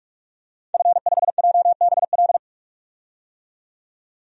Morse code, heard as a keyed tone, about 45 wpm, 700 Hz. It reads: F5JBL
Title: DXCC-100 Call Signs - 5 Characters - 45wpm